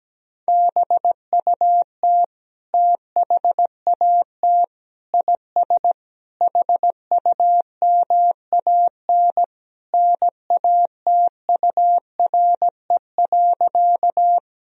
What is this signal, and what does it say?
Morse code, heard as a keyed tone, about 17 wpm, 700 Hz. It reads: BUT THAT IS HUMAN NATURE.
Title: Sentences from Top 1000 Words - Review Part 3 - 17wpm